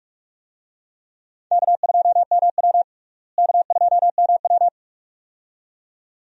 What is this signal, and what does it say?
Morse code, heard as a keyed tone, about 46 wpm, 700 Hz. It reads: K2MW K2MW
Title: USA Call Signs - 4 Characters - Repeated 2x - 45wpm